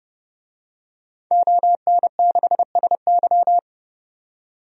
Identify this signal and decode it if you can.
Morse code, heard as a keyed tone, about 30 wpm, 700 Hz. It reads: ON6SY
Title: DXCC-100 Call Signs - 5 Characters - 30wpm